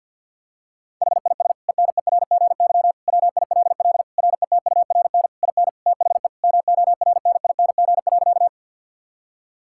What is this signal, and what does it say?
Morse code, heard as a keyed tone, about 52 wpm, 700 Hz. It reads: HIS ENERGY WILL RETURN IN THE MORNING.
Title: Sentences from Top 1000 Words - Review Part 6 - 50wpm